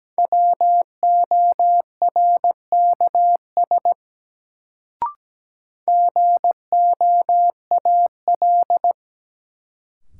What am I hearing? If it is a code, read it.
Morse code, heard as a keyed tone, about 17 wpm, 700 Hz. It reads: WORKS GOAL
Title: Sentences from Top 1000 Words - Review Part 5 - New Word Drill - 17wpm